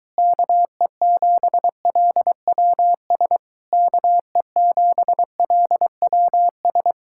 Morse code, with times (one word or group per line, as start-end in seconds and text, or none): 0.2-3.4 KE7LWS
3.7-6.9 KE7LWS